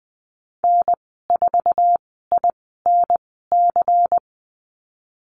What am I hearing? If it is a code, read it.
Morse code, heard as a keyed tone, about 20 wpm, spaced as for 15 wpm, 700 Hz. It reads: N4INC
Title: 140 Lesson 23g - Letter f - Call Signs 20wpm